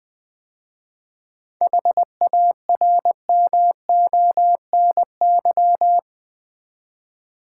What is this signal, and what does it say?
Morse code, heard as a keyed tone, about 20 wpm, 700 Hz. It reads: HARMONY
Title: 7 Letter Words - First 3 Letters Given - 20wpm